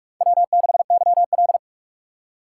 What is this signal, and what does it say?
Morse code, heard as a keyed tone, about 46 wpm, 700 Hz. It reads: W6YL
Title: USA Call Signs - 2x Character Spacing - 45wpm